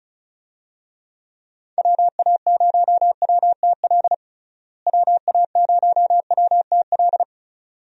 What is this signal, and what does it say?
Morse code, heard as a keyed tone, about 34 wpm, 700 Hz. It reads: WA0WTL WA0WTL
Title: USA Call Signs - 6 Characters - Repeated 2x - 35wpm